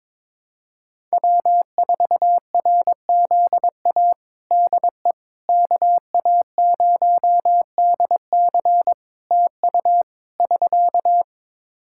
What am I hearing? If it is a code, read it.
Morse code, heard as a keyed tone, about 22 wpm, 700 Hz. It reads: W4RZA DE KA0DC TU <SK>
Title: Practice QSO - Exchange 4 - Ending - 22wpm